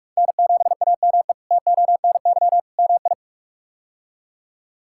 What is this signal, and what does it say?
Morse code, heard as a keyed tone, about 44 wpm, 700 Hz. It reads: N7AME TONY MI
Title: K1USN SST QSO Practice - 45wpm